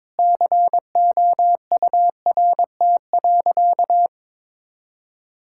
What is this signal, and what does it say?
Morse code, heard as a keyed tone, about 22 wpm, 700 Hz. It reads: COURT.
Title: Sentences from Top 1600 Words 22wpm